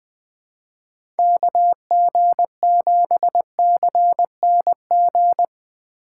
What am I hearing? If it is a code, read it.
Morse code, heard as a keyed tone, about 20 wpm, 700 Hz. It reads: KG7CNG
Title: USA Call Signs - 2x Character Spacing - 20wpm